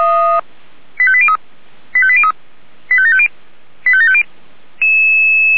Five-tone selective call (ZVEI), heard as quick runs of five tones, tones of about 70 ms. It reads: 85803 85803 85750 85750
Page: ZVEI